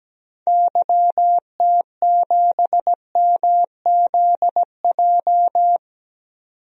Morse code, heard as a keyed tone, about 17 wpm, 700 Hz. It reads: YT7MZJ
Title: DXCC-100 Call Signs - 6 Characters - 17wpm